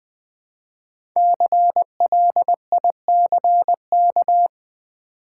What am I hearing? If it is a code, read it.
Morse code, heard as a keyed tone, about 20 wpm, 700 Hz. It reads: CLICK
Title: Sentences from Top 1100 Words - New Word Drill v2 - 20wpm